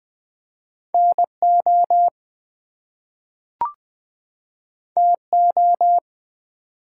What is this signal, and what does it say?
Morse code, heard as a keyed tone, about 20 wpm, 700 Hz. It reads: NO TO
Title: Two Letter Words - 20wpm